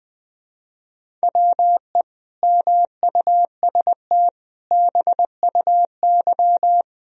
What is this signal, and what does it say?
Morse code, heard as a keyed tone, about 20 wpm, 700 Hz. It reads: WE MUST BUY
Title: Sentences from Top 1000 Words - Review Part 4 - 20wpm